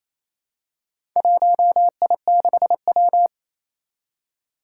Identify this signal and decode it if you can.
Morse code, heard as a keyed tone, about 28 wpm, 700 Hz. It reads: 1I6W
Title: Four Challenging Letter-Number Combinations 28wpm